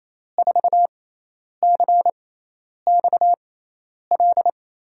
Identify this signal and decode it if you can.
Morse code, heard as a keyed tone, about 28 wpm, spaced as for 10 wpm, 700 Hz. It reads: 4CXL
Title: USA Call Signs - 6 Characters - 6x Character Spacing - 28wpm